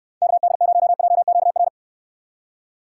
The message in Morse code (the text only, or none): KD0JQR